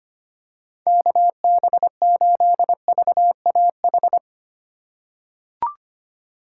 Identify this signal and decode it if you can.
Morse code, heard as a keyed tone, about 25 wpm, 700 Hz. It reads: KB8VAH
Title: USA Call Signs - 3x Character Spacing - 25wpm